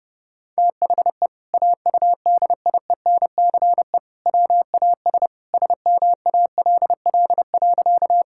Morse code, heard as a keyed tone, about 30 wpm, 700 Hz. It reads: THE AUDIENCE WAS SMALL.
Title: Sentences from Top 1000 Words - Review Part 5 - 30wpm